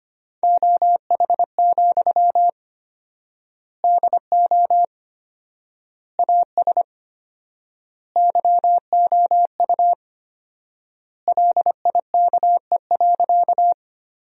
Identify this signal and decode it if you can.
Morse code, heard as a keyed tone, about 25 wpm, 700 Hz. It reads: OH, DO AS YOU LIKE.
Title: 4x Word-Spacing - Sentences from Top 500 Words - 25wpm